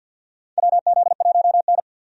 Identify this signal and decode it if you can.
Morse code, heard as a keyed tone, about 50 wpm, 700 Hz. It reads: WZ1N
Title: USA Call Signs - 4 Characters - 50wpm